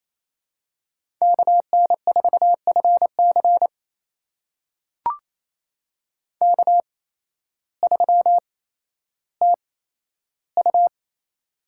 Morse code, heard as a keyed tone, about 28 wpm, spaced as for 13 wpm, 700 Hz. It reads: KN4FC K3TU
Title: USA Call Signs - 8x Character Spacing - 28wpm